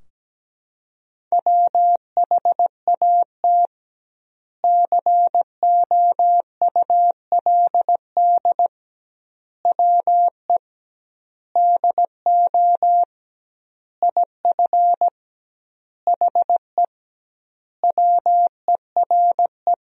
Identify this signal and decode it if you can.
Morse code, heard as a keyed tone, about 17 wpm, 700 Hz. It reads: WHAT COULD WE DO IF HE WERE
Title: 2x Word-Spacing - Sentences from Top 400 Words - 17wpm